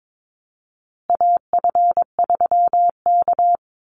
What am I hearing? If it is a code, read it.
Morse code, heard as a keyed tone, about 22 wpm, 700 Hz. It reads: AF3K
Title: Call Signs Repeated 3x - 22wpm